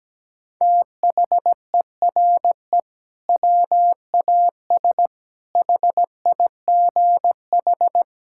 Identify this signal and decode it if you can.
Morse code, heard as a keyed tone, about 17 wpm, 700 Hz. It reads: THERE WAS HIGH
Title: Sentences from Top 1600 Words 17wpm